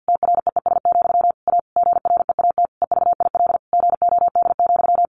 Morse code, heard as a keyed tone, about 50 wpm, 700 Hz. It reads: TREES, A GREAT EVIL GOD,
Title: A Sound Of Thunder 50wpm